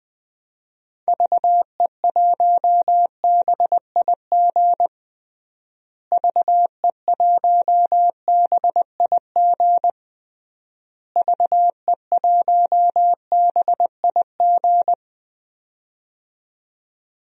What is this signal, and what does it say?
Morse code, heard as a keyed tone, about 20 wpm, 700 Hz. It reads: VE1BIG VE1BIG VE1BIG
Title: DXCC-100 Call Signs - 6 Characters - Repeated 3x - 20wpm